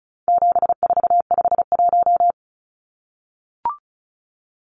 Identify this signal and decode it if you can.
Morse code, heard as a keyed tone, about 34 wpm, 700 Hz. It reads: 7451
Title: Numbers - 4-Digit 35wpm